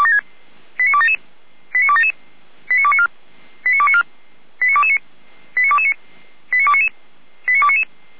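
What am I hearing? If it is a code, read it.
Five-tone selective call (ZVEI), heard as quick runs of five tones, tones of about 70 ms. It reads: 89277 89277 89294 89294 89229 89229 89299 89299
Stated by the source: ZVEI five-tone selective call received with an SDR.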